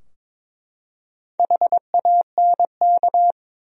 Morse code, heard as a keyed tone, about 22 wpm, 700 Hz. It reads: HANK
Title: USA Common Names - Max 4 Letters - 22wpm